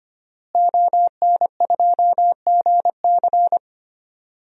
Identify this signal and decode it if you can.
Morse code, heard as a keyed tone, about 25 wpm, 700 Hz. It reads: ON2GC
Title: DXCC-100 Call Signs - 5 Characters - Repeated 3x - 25wpm